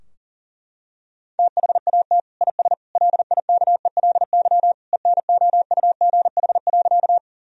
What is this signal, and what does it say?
Morse code, heard as a keyed tone, about 40 wpm, 700 Hz. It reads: THAT IS LIKELY ENOUGH.
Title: Sentences from Top 1000 Words - Review Part 4 - 40wpm